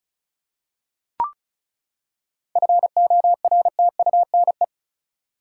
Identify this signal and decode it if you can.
Morse code, heard as a keyed tone, about 36 wpm, 700 Hz. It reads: FORTUNE
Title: Seven Letter Words - 35wpm